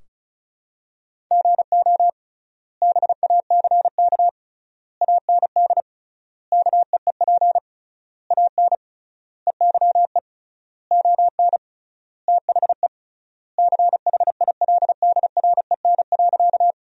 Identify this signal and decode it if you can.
Morse code, heard as a keyed tone, about 36 wpm, 700 Hz. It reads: GO BACK AND KEEP AN EYE ON THE CHILDREN.
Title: 3x Word-Spacing - Sentences from Top 300 Words - 35wpm